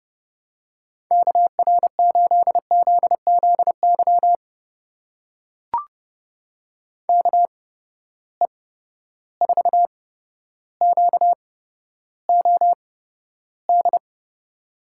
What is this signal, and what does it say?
Morse code, heard as a keyed tone, about 30 wpm, spaced as for 13 wpm, 700 Hz. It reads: KR8ZZY KE4QOD
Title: USA Call Signs - 6 Characters - 8x Character Spacing - 30wpm